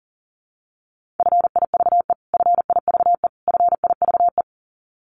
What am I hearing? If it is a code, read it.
Morse code, heard as a keyed tone, about 40 wpm, 700 Hz. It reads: FIVE FIVE FIVE
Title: Top 500 Words Repeated 40wpm